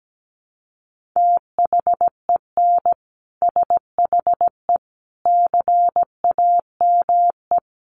Morse code, heard as a keyed tone, about 17 wpm, 700 Hz. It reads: THEN SHE CAME
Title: Sentences from Top 1300 Words 17wpm